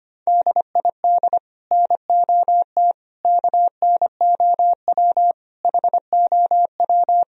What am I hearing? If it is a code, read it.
Morse code, heard as a keyed tone, about 31 wpm, 700 Hz. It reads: DID NOT KNOW HOW
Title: Speed-Racing - Sets of 5 Words 25wpm